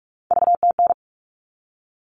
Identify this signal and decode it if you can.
Morse code, heard as a keyed tone, about 44 wpm, 700 Hz. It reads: VTN
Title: All Cut Numbers - 3-Digit 45wpm